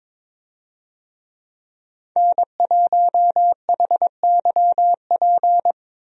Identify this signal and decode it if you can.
Morse code, heard as a keyed tone, about 22 wpm, 700 Hz. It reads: N1HYP
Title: USA Call Signs - 5 Characters - Repeated 2x - 22wpm